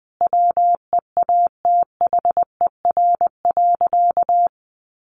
Morse code, heard as a keyed tone, about 20 wpm, 700 Hz. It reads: WEATHER.
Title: Sentences from Top 1200 Words 20wpm